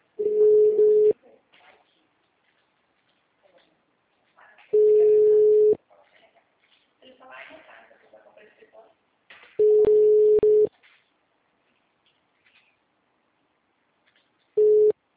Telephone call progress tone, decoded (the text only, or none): ringback tone